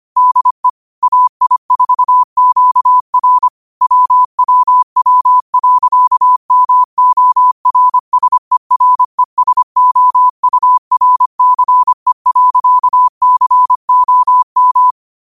Morse code, heard as a keyed tone, about 25 wpm, 1000 Hz. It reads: DE AI4QR WWW.MORSERESOURCE.COM
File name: qotdfiles / 25wpm